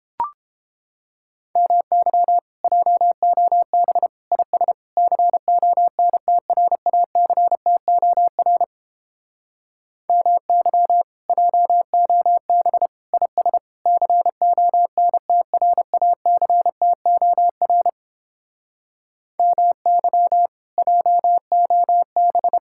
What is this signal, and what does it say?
Morse code, heard as a keyed tone, about 29 wpm, 700 Hz. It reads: MY JOB IS CONTRACTOR MY JOB IS CONTRACTOR MY JOB